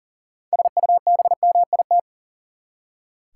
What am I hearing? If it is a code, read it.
Morse code, heard as a keyed tone, about 40 wpm, 700 Hz. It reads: SUBMIT
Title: Six Letter Words - 40wpm